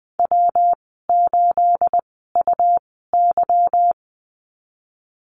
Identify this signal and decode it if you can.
Morse code, heard as a keyed tone, about 20 wpm, spaced as for 15 wpm, 700 Hz. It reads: W8UY
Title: 213 Lesson 34g - Number 8 - Call Signs 20wpm